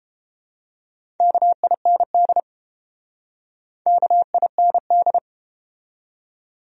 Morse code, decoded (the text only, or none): KIND KIND